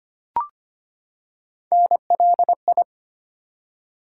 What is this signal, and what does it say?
Morse code, heard as a keyed tone, about 25 wpm, 700 Hz. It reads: NLI
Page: morse-code-ninja-practice-25wpm